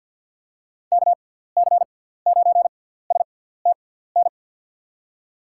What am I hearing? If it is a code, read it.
Morse code, heard as a keyed tone, about 48 wpm, spaced as for 17 wpm, 700 Hz. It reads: KC9STN
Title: USA Call Signs - 6 Characters - 6x Character Spacing - 50wpm